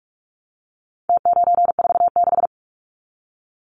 Morse code, heard as a keyed tone, about 44 wpm, 700 Hz. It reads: T946
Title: Numbers - Contest Style - 4-Digit 45wpm